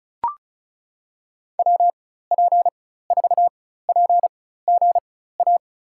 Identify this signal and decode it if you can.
Morse code, heard as a keyed tone, about 34 wpm, spaced as for 16 wpm, 700 Hz. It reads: WP4PGA
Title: USA Call Signs - 6 Characters - 4x Character Spacing - 35wpm